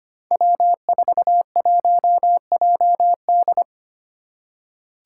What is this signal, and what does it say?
Morse code, heard as a keyed tone, about 25 wpm, 700 Hz. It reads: W41JD
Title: Five Challenging Letter-Number Combinations 25wpm